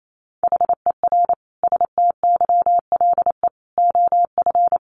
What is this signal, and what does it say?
Morse code, heard as a keyed tone, about 28 wpm, 700 Hz. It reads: HER STYLE OF
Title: Sentences from Top 800 Words 28wpm